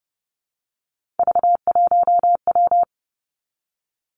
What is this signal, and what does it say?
Morse code, heard as a keyed tone, about 30 wpm, 700 Hz. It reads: V1W
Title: Three Challenging Letter-Number Combinations 30wpm